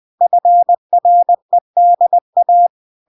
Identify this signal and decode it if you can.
Morse code, heard as a keyed tone, about 20 wpm, 700 Hz. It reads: FREDA